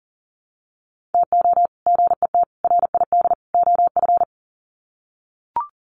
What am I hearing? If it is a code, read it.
Morse code, heard as a keyed tone, about 40 wpm, 700 Hz. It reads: TO GET RID OF
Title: Sets of 4 Words 40wpm